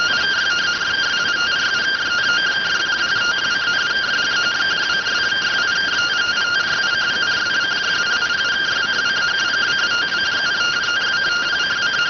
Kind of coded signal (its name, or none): two-tone teleprinter signal